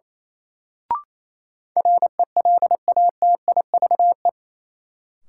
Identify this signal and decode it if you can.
Morse code, heard as a keyed tone, about 28 wpm, 700 Hz. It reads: RELATIVE
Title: Sentences from Top 1400 Words - New Word Drill Alt - 28wpm